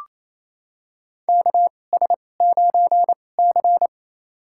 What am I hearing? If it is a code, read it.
Morse code, heard as a keyed tone, about 27 wpm, spaced as for 20 wpm, 690 Hz. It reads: KS9C
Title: USA Call Signs - 4 Characters - 2x Character Spacing - 28wpm